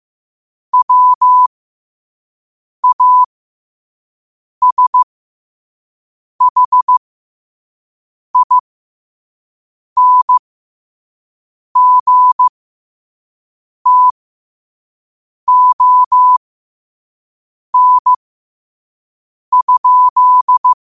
Morse code, heard as a keyed tone, about 15 wpm, spaced as for 5 wpm, 1000 Hz. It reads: WASHINGTON?
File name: qotdfiles / 5wpm